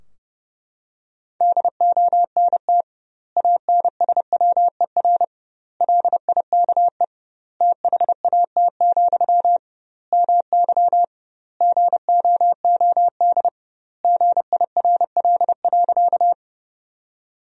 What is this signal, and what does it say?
Morse code, heard as a keyed tone, about 30 wpm, 700 Hz. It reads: DONT ANSWER LIKE THAT, MY GOOD GIRL.
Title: 2x Word-Spacing - Sentences from Top 400 Words - 30wpm